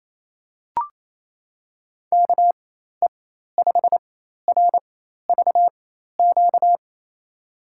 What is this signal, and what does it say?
Morse code, heard as a keyed tone, about 28 wpm, spaced as for 13 wpm, 700 Hz. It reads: KE5RVQ
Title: USA Call Signs - 6 Characters - 4x Character Spacing - 28wpm